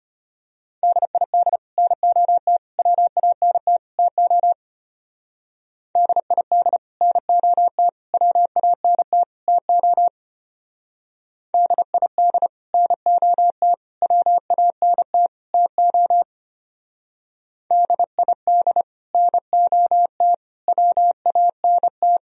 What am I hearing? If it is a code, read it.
Morse code, heard as a keyed tone, about 31 wpm, 700 Hz. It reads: DID NOT WANT TO DID NOT WANT TO DID NOT WANT TO DID NOT WANT
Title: Speed-Racing - Sets of 4 Words 25wpm